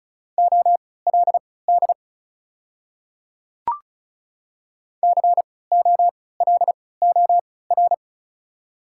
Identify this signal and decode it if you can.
Morse code, heard as a keyed tone, about 34 wpm, spaced as for 20 wpm, 700 Hz. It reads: OLD COLOR
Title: 3x Character Spacing - Top 300 Words - 35wpm